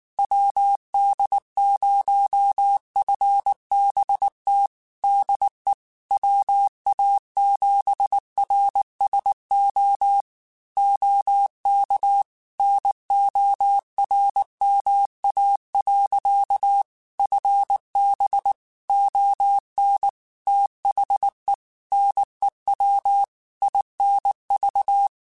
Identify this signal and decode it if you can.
Morse code, heard as a keyed tone, about 19 wpm, 800 Hz. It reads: WD0FBT DE WA7RSO OK NORMA. FB ON THE NEW INV
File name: QSOs / QSOs-mp3s / WD0FBT 1@18